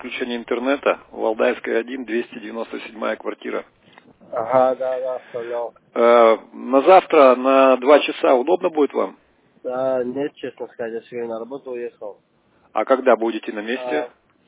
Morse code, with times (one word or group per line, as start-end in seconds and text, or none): none